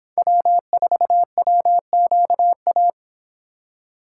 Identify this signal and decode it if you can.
Morse code, heard as a keyed tone, about 26 wpm, 700 Hz. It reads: W4WQA